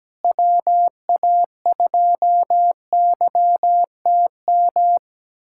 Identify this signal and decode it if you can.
Morse code, heard as a keyed tone, about 17 wpm, 700 Hz. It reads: WA2YTM
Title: USA Callsign DE Callsign - 6 Characters - 17wpm